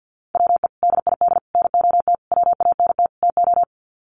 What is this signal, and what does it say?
Morse code, heard as a keyed tone, about 52 wpm, 700 Hz. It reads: WE DID NOT WANT TO
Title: Sets of 5 Words Encore 50wpm